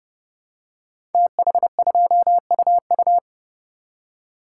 Morse code, heard as a keyed tone, about 30 wpm, 700 Hz. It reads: TH2UU
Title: Code Groups - 5 Letter:Numbers - 30wpm